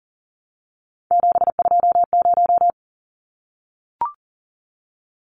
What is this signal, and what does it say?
Morse code, heard as a keyed tone, about 40 wpm, 700 Hz. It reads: 720
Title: Numbers - 3-Digit 40wpm